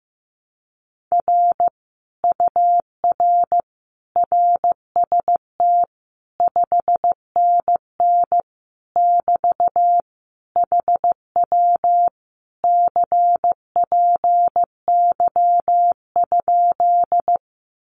Morse code, heard as a keyed tone, about 15 wpm, 700 Hz. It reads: R UR RST 5NN <BT> HW CPY?
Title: Common QSO Phrases Encore 15wpm